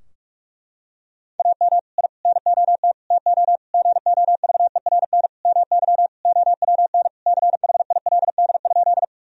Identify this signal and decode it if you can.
Morse code, heard as a keyed tone, about 46 wpm, 700 Hz. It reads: AM I NOT TO GOVERN MY OWN CHILD?
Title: Sentences from Top 1000 Words - Review Part 2 - 45wpm